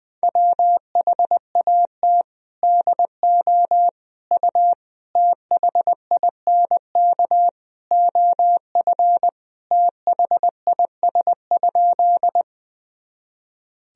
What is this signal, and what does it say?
Morse code, heard as a keyed tone, about 21 wpm, 700 Hz. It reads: WHAT DO U THINK OF THIS?
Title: Sentences from Top 200 Words and Abbreviations 20wpm